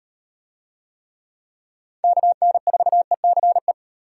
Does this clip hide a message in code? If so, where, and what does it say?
Yes: Morse code from 2.0 to 3.7 s, heard as a keyed tone, about 36 wpm, 700 Hz. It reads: KN4ECE